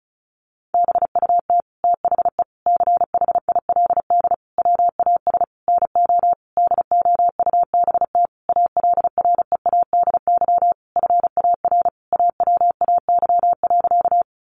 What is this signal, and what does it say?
Morse code, heard as a keyed tone, about 34 wpm, 700 Hz. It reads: BUT THE CHILD WAS NO DOUBT ALREADY FAR AWAY.
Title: Sentences from Top 1500 Words 35wpm